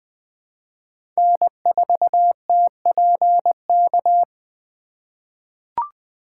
Morse code, heard as a keyed tone, about 20 wpm, 700 Hz. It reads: N4TPK
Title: USA Call Signs - 5 Characters - 3x Character Spacing - 20wpm